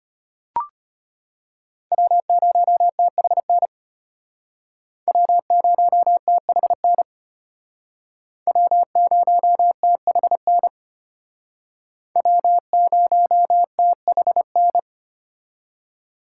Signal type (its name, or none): Morse code